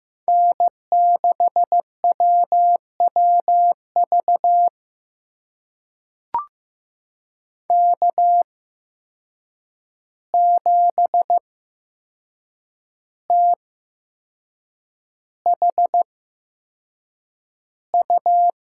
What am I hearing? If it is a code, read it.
Morse code, heard as a keyed tone, about 15 wpm, spaced as for 6 wpm, 700 Hz. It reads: N6WWV K7THU
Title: USA Call Signs - 5 Characters - 8x Character Spacing - 15wpm